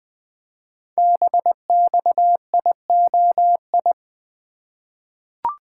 Morse code, heard as a keyed tone, about 20 wpm, 700 Hz. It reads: BXIOI
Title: Code Groups - 5 Letters - 20wpm